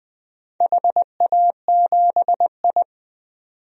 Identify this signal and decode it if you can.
Morse code, heard as a keyed tone, about 21 wpm, 700 Hz. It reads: HA7I
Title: DXCC-100 Call Signs - 4 Characters - 20wpm